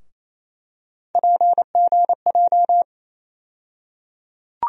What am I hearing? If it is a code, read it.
Morse code, heard as a keyed tone, about 27 wpm, 700 Hz. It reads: PGJ
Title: USA Call Sign Postfixes - 3 Letters 28wpm